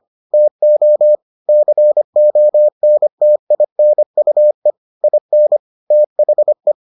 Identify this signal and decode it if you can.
Morse code, heard as a keyed tone, about 25 wpm, 600 Hz. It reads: TO CONTINUE IN THE